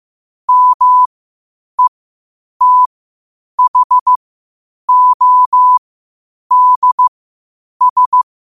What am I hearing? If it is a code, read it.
Morse code, heard as a keyed tone, about 15 wpm, spaced as for 8 wpm, 1000 Hz. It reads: METHODS
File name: qotdfiles / 8wpm